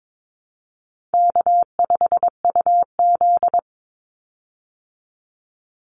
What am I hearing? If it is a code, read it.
Morse code, heard as a keyed tone, about 22 wpm, 700 Hz. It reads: K5UZ
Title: Call Signs Repeated 2x - 22wpm